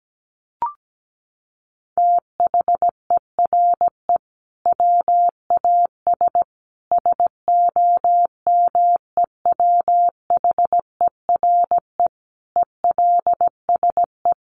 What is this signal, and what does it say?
Morse code, heard as a keyed tone, about 17 wpm, 700 Hz. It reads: THERE WAS SOMEWHERE ELSE
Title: Sentences from Top 1400 Words 17wpm